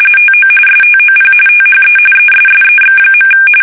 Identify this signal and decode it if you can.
RTTY, about 45.45 baud. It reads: TO STOP THE ESSEX.
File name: fsk850 / usnavy2